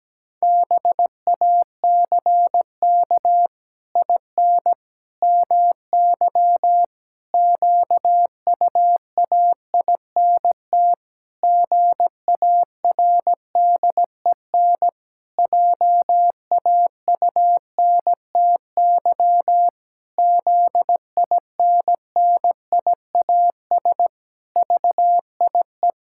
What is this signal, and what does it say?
Morse code, heard as a keyed tone, about 17 wpm, 700 Hz. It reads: BACK IN MY QUAINT GARDEN JAUNTY ZINNIAS VIE
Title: Pangram Challenge v2 - 17wpm